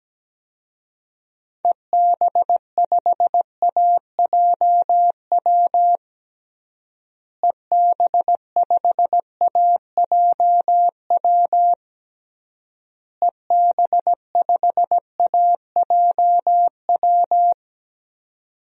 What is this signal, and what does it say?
Morse code, heard as a keyed tone, about 17 wpm, 700 Hz. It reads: EB5AJW EB5AJW EB5AJW
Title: DXCC-100 Call Signs - 6 Characters - Repeated 3x - 17wpm